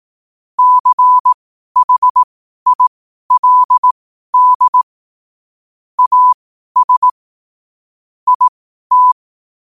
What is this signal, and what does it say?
Morse code, heard as a keyed tone, about 18 wpm, spaced as for 12 wpm, 1000 Hz. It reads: CHILD AS IT
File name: qotdfiles / 12wpm